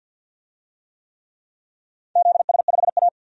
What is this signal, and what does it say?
Morse code, heard as a keyed tone, about 51 wpm, 700 Hz. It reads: ZS5A
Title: DXCC-100 Call Signs - 4 Characters - 50wpm